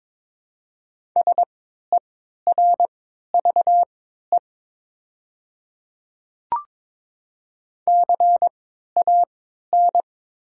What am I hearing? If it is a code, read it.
Morse code, heard as a keyed tone, about 22 wpm, spaced as for 13 wpm, 700 Hz. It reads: SERVE CAN
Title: 3x Character Spacing - Top 500 Words - 22wpm